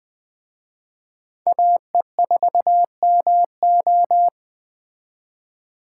Morse code, heard as a keyed tone, about 20 wpm, 700 Hz. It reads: AE4MO
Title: Speed Racing - USA 2x2 Call Signs 17wpm